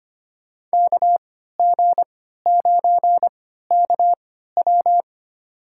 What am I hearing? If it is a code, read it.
Morse code, heard as a keyed tone, about 25 wpm, spaced as for 14 wpm, 700 Hz. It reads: KG9KW
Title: USA Call Signs - 5 Characters - 3x Character Spacing - 25wpm